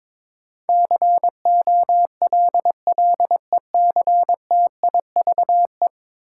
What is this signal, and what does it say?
Morse code, heard as a keyed tone, about 22 wpm, 700 Hz. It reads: COLLECTIVE
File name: practice / 22wpm / 10 Letter Words - First 3 Letters Given - 22wpm